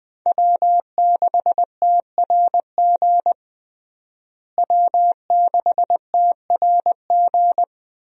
Morse code, heard as a keyed tone, about 20 wpm, 700 Hz. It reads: W6TRG W6TRG
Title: USA Call Signs - 5 Characters - Repeated 2x - 20wpm